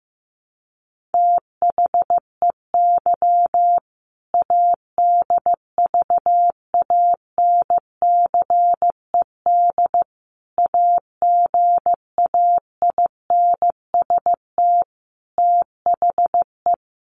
Morse code, heard as a keyed tone, about 15 wpm, 700 Hz. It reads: THEY ADVANCED AGAINST THE
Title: Sentences from Top 1200 Words 15wpm